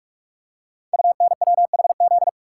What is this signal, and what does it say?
Morse code, heard as a keyed tone, about 44 wpm, 690 Hz. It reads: UNWHZ
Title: Code Groups - 5 Letters - 45wpm